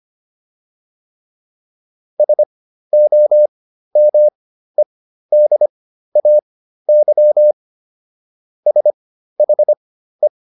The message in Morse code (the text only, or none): SOMEDAY SHE